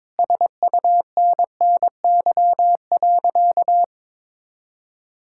Morse code, heard as a keyed tone, about 22 wpm, 700 Hz. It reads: SUNNY.
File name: practice / 22wpm / Sentences with 2-Word Phrases - 22wpm